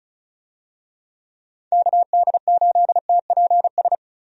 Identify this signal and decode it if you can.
Morse code, heard as a keyed tone, about 34 wpm, 700 Hz. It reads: KD8TPS
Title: USA Call Signs - 6 Characters - Repeated 2x - 35wpm